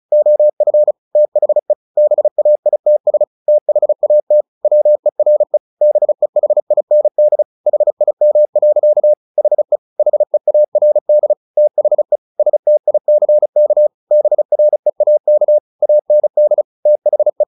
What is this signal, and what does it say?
Morse code, heard as a keyed tone, about 34 wpm, 600 Hz. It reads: OF THE BAITS THAT WERE BEHIND HIM. HE HEARD THE STICK BREAK AND THE
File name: the old man and the sea / the old man and the sea 1 35wpm